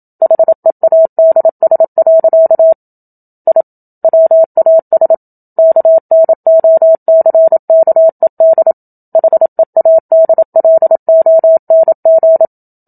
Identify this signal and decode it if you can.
Morse code, heard as a keyed tone, about 27 wpm, 650 Hz. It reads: HEADS. I WAS KNOCKED HEADLONG
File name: Book Two / 01-Chapter 1 - 27 WPM